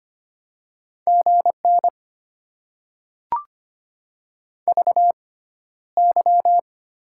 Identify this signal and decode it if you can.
Morse code, heard as a keyed tone, about 25 wpm, spaced as for 13 wpm, 700 Hz. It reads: GN VY
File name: Top 90 QSO Elements - 6x Character Spacing - 25wpm